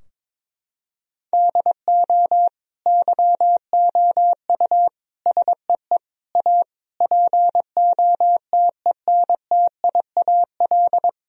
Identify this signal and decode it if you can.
Morse code, heard as a keyed tone, about 22 wpm, 700 Hz. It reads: DO YOU SEE A POTENTIAL
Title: Sentences from Top 1000 Words - Review Part 4 - 22wpm